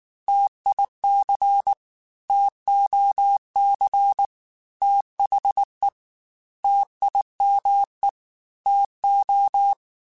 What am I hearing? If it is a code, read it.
Morse code, heard as a keyed tone, about 19 wpm, 800 Hz. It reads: TIC TOC THE TIME TO